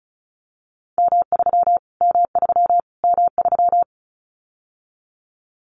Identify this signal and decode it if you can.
Morse code, heard as a keyed tone, about 34 wpm, 700 Hz. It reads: M3 M3 M3
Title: Callsign Prefixes for Top 25 DXCC - Repeated 3x - 35wpm